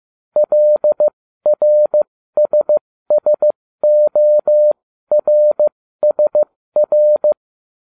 Morse code, heard as a keyed tone, about 15 wpm, spaced as for 13 wpm, 600 Hz. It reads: LRSSORSR